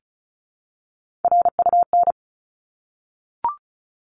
Morse code, heard as a keyed tone, about 34 wpm, 700 Hz. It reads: RUN
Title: Top 300 Words 35wpm